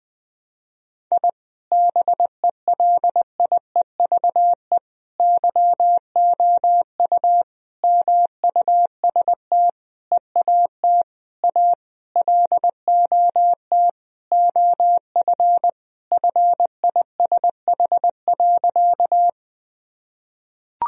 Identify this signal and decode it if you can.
Morse code, heard as a keyed tone, about 20 wpm, 700 Hz. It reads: I BELIEVE YOU MUST EAT A LOT OF FISH.
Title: Sentences from Top 1000 Words - Review Part 3 - 20wpm